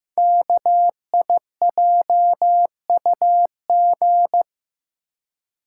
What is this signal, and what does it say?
Morse code, heard as a keyed tone, about 15 wpm, 700 Hz. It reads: KIJUG
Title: Code Groups - 5 Letters - 15wpm